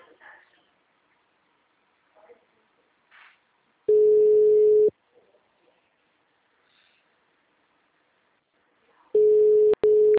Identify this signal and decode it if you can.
Ringback tone (Europe, 425 Hz).
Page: Ribeirao sala 906